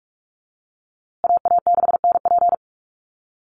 Morse code, heard as a keyed tone, about 44 wpm, 700 Hz. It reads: AA6NP
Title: USA Call Signs Repeated 3x 45wpm